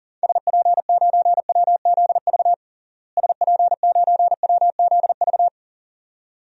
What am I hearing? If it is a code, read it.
Morse code, heard as a keyed tone, about 37 wpm, 700 Hz. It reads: SP9WZV SP9WZV
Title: DXCC-100 Call Signs - 6 Characters - Repeated 2x - 40wpm